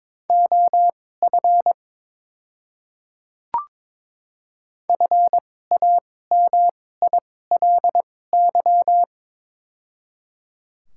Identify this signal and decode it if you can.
Morse code, heard as a keyed tone, about 22 wpm, spaced as for 16 wpm, 700 Hz. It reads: OF FAMILY
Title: 2x Character Spacing - Top 500 Words - 22wpm